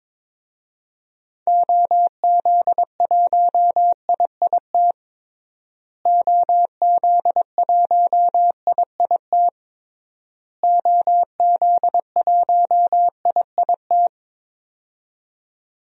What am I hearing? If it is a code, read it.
Morse code, heard as a keyed tone, about 22 wpm, 700 Hz. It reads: OZ1IIT OZ1IIT OZ1IIT
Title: DXCC-100 Call Signs - 6 Characters - Repeated 3x - 22wpm